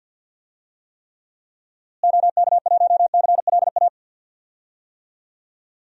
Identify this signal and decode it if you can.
Morse code, heard as a keyed tone, about 47 wpm, 700 Hz. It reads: OK1CLA
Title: DXCC-100 Call Signs - 6 Characters - Repeated 3x - 50wpm